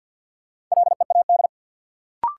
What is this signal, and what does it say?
Morse code, heard as a keyed tone, about 52 wpm, 700 Hz. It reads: LEAD
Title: Sentences from Top 500 Words - New Word Drill - 50wpm